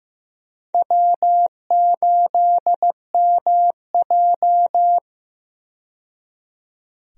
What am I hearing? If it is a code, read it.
Morse code, heard as a keyed tone, about 15 wpm, 700 Hz. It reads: W8MJ
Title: USA Call Signs - 4 Characters - 15wpm